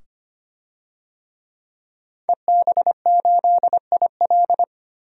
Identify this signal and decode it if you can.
Morse code, heard as a keyed tone, about 25 wpm, 700 Hz. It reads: EB8IL
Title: DXCC-100 Call Signs - 5 Characters - 25wpm